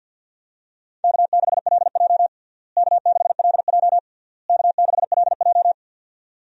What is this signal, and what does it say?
Morse code, heard as a keyed tone, about 53 wpm, 700 Hz. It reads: K6LJ K6LJ K6LJ
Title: USA Call Signs - 4 Characters - Repeated 3x - 50wpm